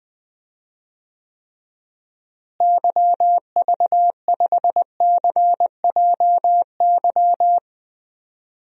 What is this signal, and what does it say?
Morse code, heard as a keyed tone, about 21 wpm, 700 Hz. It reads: YV5CJY
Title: DXCC-100 Call Signs - 6 Characters - Repeated 2x - 20wpm